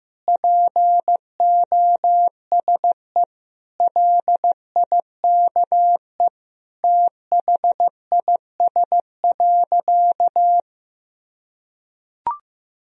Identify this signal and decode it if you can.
Morse code, heard as a keyed tone, about 15 wpm, 700 Hz. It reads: POSE LIKE THIS.
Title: Sentences from Top 1000 Words - Review Part 2 - 15wpm